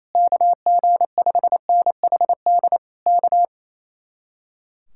Morse code, heard as a keyed tone, about 27 wpm, 700 Hz. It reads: KG5NHD K
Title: Speed-Racing Common QSO Phrases Encore - 28wpm